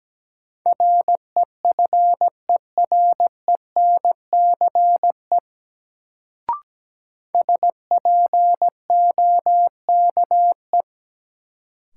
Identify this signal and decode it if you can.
Morse code, heard as a keyed tone, about 17 wpm, 700 Hz. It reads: REFERENCE SPOKE
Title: Sentences from Top 1200 Words - New Word Drill - 17wpm